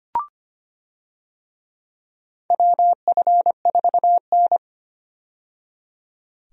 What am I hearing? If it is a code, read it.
Morse code, heard as a keyed tone, about 25 wpm, 700 Hz. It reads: WF4N
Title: USA Call Signs - 4 Characters - 25wpm